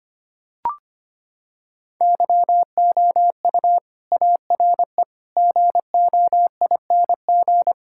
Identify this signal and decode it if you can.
Morse code, heard as a keyed tone, about 25 wpm, 700 Hz. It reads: YOU ARE GOING
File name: practice / 25wpm / Sentences with 4-Word Phrases - 25wpm